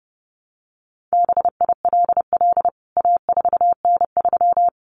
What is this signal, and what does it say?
Morse code, heard as a keyed tone, about 30 wpm, 700 Hz. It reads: BILL A4N3
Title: CWT Exchange 30wpm